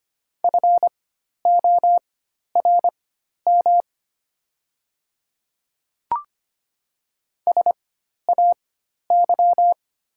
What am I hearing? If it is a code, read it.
Morse code, heard as a keyed tone, about 25 wpm, spaced as for 12 wpm, 690 Hz. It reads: FORM SAY
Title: 4x Character Spacing - Top 200 Words - 25wpm